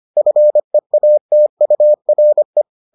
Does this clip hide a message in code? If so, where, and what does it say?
Yes: Morse code from 0.2 to 2.6 s, heard as a keyed tone, about 25 wpm, 600 Hz. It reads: FEATURE